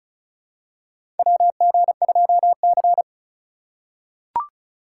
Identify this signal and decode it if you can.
Morse code, heard as a keyed tone, about 36 wpm, 700 Hz. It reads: WG2C
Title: USA Call Signs - 3x Character Spacing - 35wpm